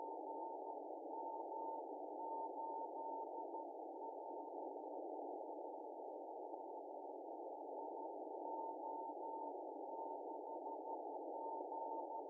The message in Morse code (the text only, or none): none